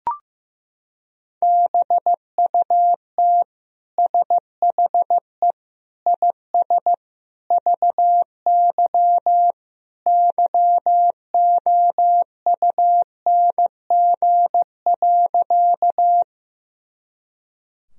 Morse code, heard as a keyed tone, about 14 wpm, 700 Hz. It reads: BUT SHE IS VY YOUNG.
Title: Sentences from Top 400 Words and Abbreviations 15wpm